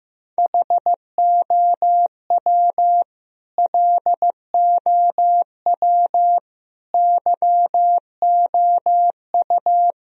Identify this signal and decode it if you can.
Morse code, heard as a keyed tone, about 15 wpm, 700 Hz. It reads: HOW LOW YOU
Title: Sentences from Top 1000 Words - Review Part 1 - 15wpm